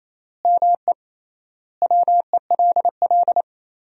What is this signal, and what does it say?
Morse code, heard as a keyed tone, about 28 wpm, 700 Hz. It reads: ME WELL
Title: 3x Word-Spacing - Sentences from Top 400 Words - 28wpm